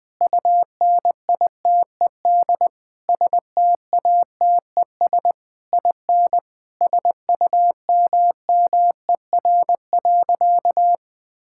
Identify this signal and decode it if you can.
Morse code, heard as a keyed tone, about 21 wpm, 700 Hz. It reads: UNITED STATES IN SUMMER.
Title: Sentences with 4-Word Phrases - 20wpm